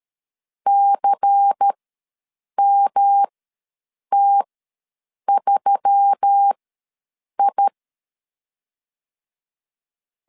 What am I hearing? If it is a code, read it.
Morse code, heard as a keyed tone, about 13 wpm, spaced as for 7 wpm, 800 Hz. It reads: CMT3I
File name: LEZIONI / 7 GF9Z / 7 GF ws 12wpm 3140